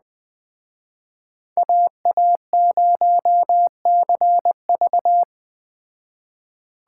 Morse code, heard as a keyed tone, about 20 wpm, 700 Hz. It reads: AA0CV
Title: Speed Racing - USA 2x2 Call Signs 17wpm